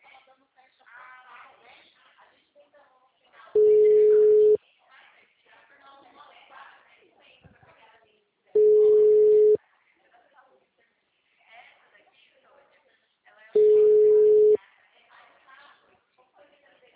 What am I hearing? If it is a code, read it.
Ringback tone (Europe, 425 Hz), with speech.